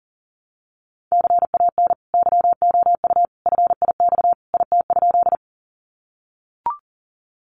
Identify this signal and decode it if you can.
Morse code, heard as a keyed tone, about 40 wpm, 700 Hz. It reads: CAN YOU FIX IT?
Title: Sentences from Top 1400 Words 40wpm